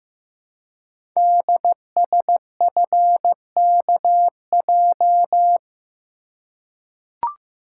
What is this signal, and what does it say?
Morse code, heard as a keyed tone, about 15 wpm, 700 Hz. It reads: DSFKJ
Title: Code Groups - 5 Letters - 15wpm